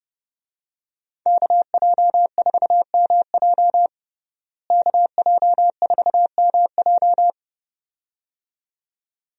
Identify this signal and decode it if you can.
Morse code, heard as a keyed tone, about 32 wpm, 700 Hz. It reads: KJ4MJ KJ4MJ
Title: USA Call Signs - 5 Characters - Repeated 2x - 30wpm